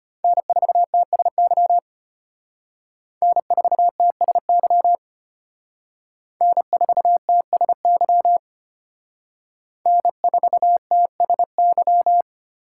Morse code, heard as a keyed tone, about 32 wpm, 700 Hz. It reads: N4TSY N4TSY N4TSY N4TSY